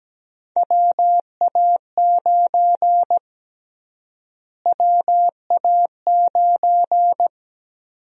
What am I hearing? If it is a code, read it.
Morse code, heard as a keyed tone, about 17 wpm, 700 Hz. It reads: WA9 WA9
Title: USA Call Sign Prefixes - Repeated 2x - 17wpm